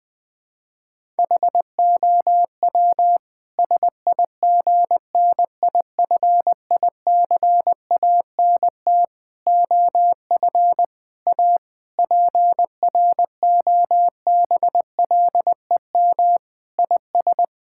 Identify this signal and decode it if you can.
Morse code, heard as a keyed tone, about 20 wpm, 700 Hz. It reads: HOW SIGNIFICANT OF A PROBLEM IS
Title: Sentences from Top 1000 Words - Review Part 6 - 20wpm